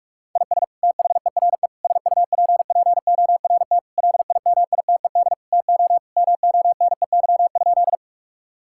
Morse code, heard as a keyed tone, about 46 wpm, 700 Hz. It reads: IS THERE SUPPORT LIMITED TO MONEY?
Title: Sentences from Top 1000 Words - Review Part 6 - 45wpm